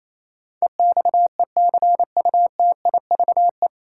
Morse code, heard as a keyed tone, about 28 wpm, 700 Hz. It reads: EXECUTIVE
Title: Sets of 5 Words Encore - New Word Drill - 28wpm